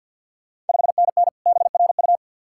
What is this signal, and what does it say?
Morse code, heard as a keyed tone, about 49 wpm, 690 Hz. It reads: 5NN BRU
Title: Ontario QSO Party Exchange 50wpm